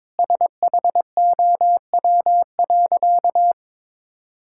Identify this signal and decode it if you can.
Morse code, heard as a keyed tone, about 21 wpm, 700 Hz. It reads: SHOW.
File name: practice / 22wpm / Easy Sentences with Hints - 22wpm